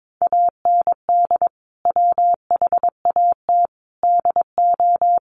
Morse code, heard as a keyed tone, about 22 wpm, 700 Hz. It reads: AND WHAT DO
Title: Sentences from Top 1100 Words 22wpm